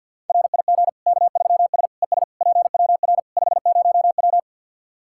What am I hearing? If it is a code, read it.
Morse code, heard as a keyed tone, about 52 wpm, 700 Hz. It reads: RIG K3S ES PWR 50W
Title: K1USN SST QSO Practice 50wpm